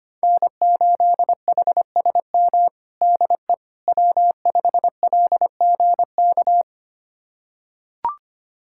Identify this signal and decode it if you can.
Morse code, heard as a keyed tone, about 25 wpm, 700 Hz. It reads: N8HSM DE W5LGK
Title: USA Callsign DE Callsign - 5 Characters - 25wpm